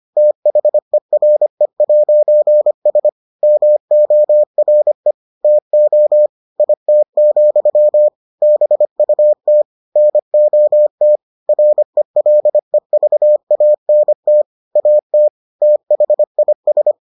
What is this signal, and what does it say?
Morse code, heard as a keyed tone, about 25 wpm, 600 Hz. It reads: THERE'S MORE TO IT, BUT NOT RELEVANT AT THIS